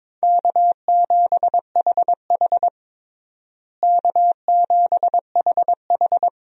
Morse code, heard as a keyed tone, about 22 wpm, 700 Hz. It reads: K7HH K7HH
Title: USA Call Signs - 4 Characters - Repeated 2x - 22wpm